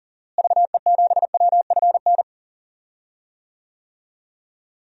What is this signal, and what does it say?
Morse code, heard as a keyed tone, about 40 wpm, 700 Hz. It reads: VE7WFN
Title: DXCC-100 Call Signs - 6 Characters - Repeated 2x - 40wpm